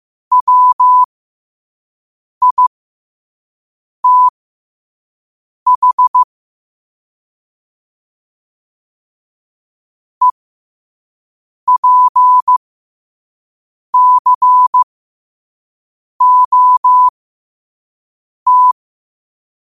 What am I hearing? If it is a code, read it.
Morse code, heard as a keyed tone, about 15 wpm, spaced as for 5 wpm, 1000 Hz. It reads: WITH EPCOT
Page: Quotes for Fri, 15 Aug 2025 in Morse Code at 5 words per minute.